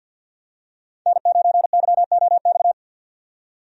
Morse code, heard as a keyed tone, about 51 wpm, 700 Hz. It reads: N9YOX
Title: USA Call Signs - 5 Characters - 50wpm